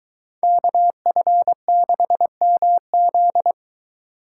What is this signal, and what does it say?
Morse code, heard as a keyed tone, about 23 wpm, 700 Hz. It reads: KF6MZ